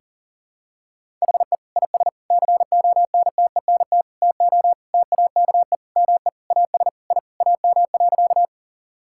Morse code, heard as a keyed tone, about 40 wpm, 700 Hz. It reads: HE IS CONTENT TO TAKE ME AS I AM.
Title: Sentences from Top 1000 Words - Review Part 6 - 40wpm